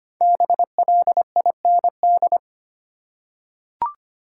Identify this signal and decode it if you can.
Morse code, heard as a keyed tone, about 25 wpm, 700 Hz. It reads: BLIND
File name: 5 Letter Words - First 3 Letters Given - 25wpm